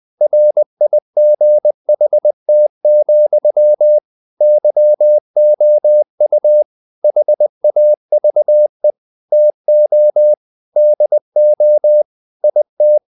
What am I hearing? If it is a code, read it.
Morse code, heard as a keyed tone, about 20 wpm, 600 Hz. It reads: RIGHT, YOU HAVE TO DO IT
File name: CW / Practice / EnglishSayings20